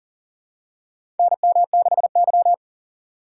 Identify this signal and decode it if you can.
Morse code, heard as a keyed tone, about 37 wpm, 700 Hz. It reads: NM6Y
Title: USA Call Signs - 6x Character Spacing - 40wpm